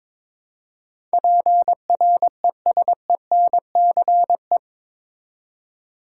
Morse code, heard as a keyed tone, about 22 wpm, 700 Hz. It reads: PRESENCE
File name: practice / 22wpm / Eight Letter Words - 22wpm